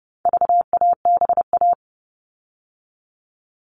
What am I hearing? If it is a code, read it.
Morse code, heard as a keyed tone, about 30 wpm, 700 Hz. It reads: VABA
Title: All Cut Numbers - 4-Digit 30wpm